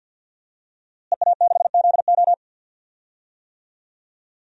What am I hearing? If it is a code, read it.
Morse code, heard as a keyed tone, about 47 wpm, 700 Hz. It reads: EA6ZO
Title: DXCC-100 Call Signs - 5 Characters - Repeated 3x - 50wpm